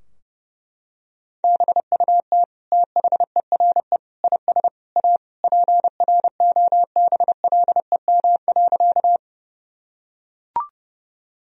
Morse code, heard as a keyed tone, about 32 wpm, 700 Hz. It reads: BUT THERE IS A PROBLEM.
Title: Sentences from Top 1000 Words - Review Part 2 - 30wpm